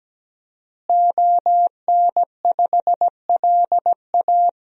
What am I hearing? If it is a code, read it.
Morse code, heard as a keyed tone, about 17 wpm, 700 Hz. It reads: ON5LA
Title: DXCC-100 Call Signs - 5 Characters - Repeated 3x - 17wpm